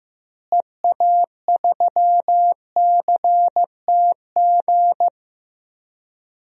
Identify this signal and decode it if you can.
Morse code, heard as a keyed tone, about 15 wpm, 700 Hz. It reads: EA3CTG
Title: DXCC-100 Call Signs - 6 Characters - Repeated 3x - 15wpm